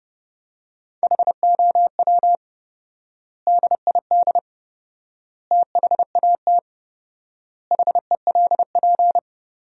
Morse code, heard as a keyed tone, about 30 wpm, 700 Hz. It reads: HOW DID THAT HELP
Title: 4x Word-Spacing - Sentences from Top 200 Words - 30wpm